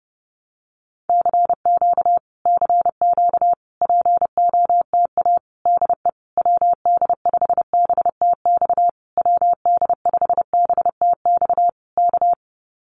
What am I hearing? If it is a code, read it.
Morse code, heard as a keyed tone, about 30 wpm, 700 Hz. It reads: CQ CQ POTA DE WD5BTX WD5BTX K
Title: Common QSO Phrases Encore 30wpm